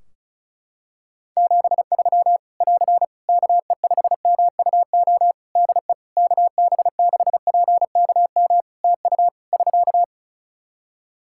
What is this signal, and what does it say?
Morse code, heard as a keyed tone, about 36 wpm, 700 Hz. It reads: 73 <AR> KE5MUO DE KB6PKM TU <SK>
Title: Practice QSO - Exchange 4 - Ending - 35wpm